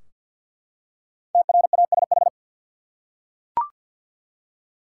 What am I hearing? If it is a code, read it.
Morse code, heard as a keyed tone, about 47 wpm, 700 Hz. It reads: TRASH
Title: Five Letter Words - 50wpm